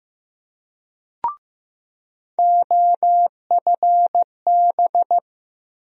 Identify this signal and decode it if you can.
Morse code, heard as a keyed tone, about 16 wpm, 700 Hz. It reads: OFB
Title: USA Call Sign Postfixes - 3 Letters 15wpm